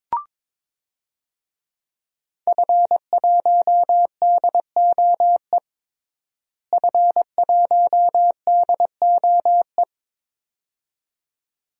Morse code, heard as a keyed tone, about 22 wpm, 700 Hz. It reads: F1DOE F1DOE
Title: DXCC-100 Call Signs - 5 Characters - Repeated 2x - 22wpm